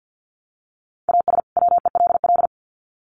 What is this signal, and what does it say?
Morse code, heard as a keyed tone, about 50 wpm, 700 Hz. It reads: AS WELL
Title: Sets of 2 Words 50wpm